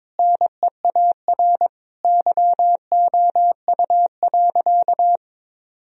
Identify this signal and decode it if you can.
Morse code, heard as a keyed tone, about 22 wpm, 700 Hz. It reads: NEAR YOU.
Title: Sentences from Top 1000 Words - Review Part 1 - 22wpm